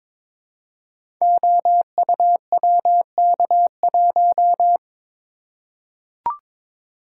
Morse code, heard as a keyed tone, about 23 wpm, 700 Hz. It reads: OUWK1
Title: Code Groups - 5 Letter:Numbers - 22wpm